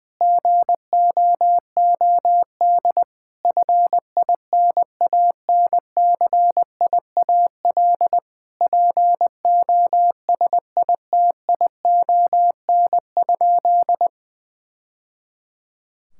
Morse code, heard as a keyed tone, about 20 wpm, 700 Hz. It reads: GOOD FINANCIAL POSITION?
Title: Sentences from Top 1000 Words - Review Part 5 - 20wpm